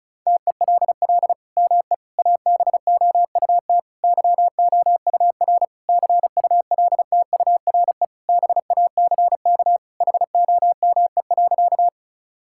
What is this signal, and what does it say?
Morse code, heard as a keyed tone, about 36 wpm, 700 Hz. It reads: TELL ME ABOUT YOUR CULTURE BACK HOME.
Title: Sentences from Top 1000 Words - Review Part 6 - 35wpm